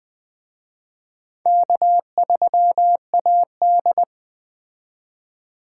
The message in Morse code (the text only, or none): K3AD